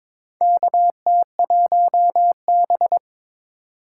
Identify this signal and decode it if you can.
Morse code, heard as a keyed tone, about 22 wpm, 700 Hz. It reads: KT1B
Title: K1USN SST QSO Practice - 22wpm